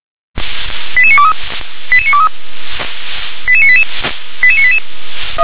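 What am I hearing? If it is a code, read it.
Five-tone selective call (ZVEI), heard as quick runs of five tones, tones of about 70 ms. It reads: 88923 88923 88988 88988
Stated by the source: ZVEI five-tone selective call received on an SDR.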